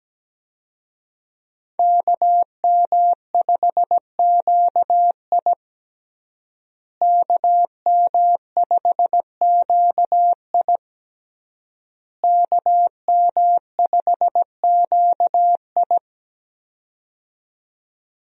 Morse code, heard as a keyed tone, about 17 wpm, 700 Hz. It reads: KM5QI KM5QI KM5QI
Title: USA Call Signs - 5 Characters - Repeated 3x - 17wpm